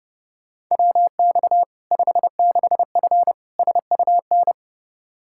Morse code, heard as a keyed tone, about 30 wpm, 700 Hz. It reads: WX 56F SUN
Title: Practice QSO - Exchange 2 - Info - 30wpm